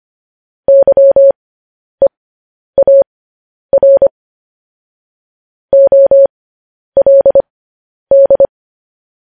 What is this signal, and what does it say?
Morse code, heard as a keyed tone, about 25 wpm, spaced as for 10 wpm, 550 Hz. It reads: YEAR OLD
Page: news headlines in morse code at 10 wpm